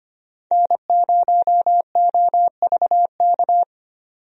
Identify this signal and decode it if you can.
Morse code, heard as a keyed tone, about 25 wpm, 700 Hz. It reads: N0OVK
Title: USA Call Signs - 5 Characters - 25wpm